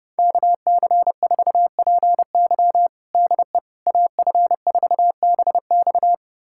Morse code, heard as a keyed tone, about 31 wpm, 700 Hz. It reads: KC4PY DE AF4BX
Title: USA Callsign DE Callsign - 5 Characters - 30wpm